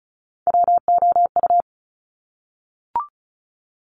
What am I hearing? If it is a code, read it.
Morse code, heard as a keyed tone, about 36 wpm, 700 Hz. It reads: WOU
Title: Top Three Letter Combinations in Words 35wpm